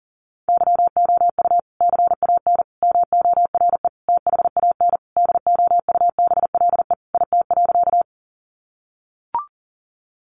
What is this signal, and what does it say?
Morse code, heard as a keyed tone, about 40 wpm, 700 Hz. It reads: YOU CAN MORE THAN DOUBLE IT.
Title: Sentences from Top 1400 Words 40wpm